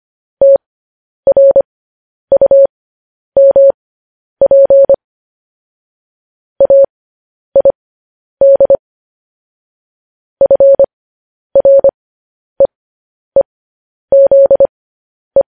Morse code, heard as a keyed tone, about 25 wpm, spaced as for 10 wpm, 550 Hz. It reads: TRUMP AID FREEZE
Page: news-headlines-in-morse-code-at-10-wpm